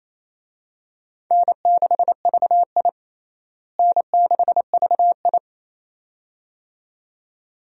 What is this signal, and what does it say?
Morse code, heard as a keyed tone, about 27 wpm, 700 Hz. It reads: N6VI N6VI
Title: USA Call Signs - 4 Characters - Repeated 2x - 28wpm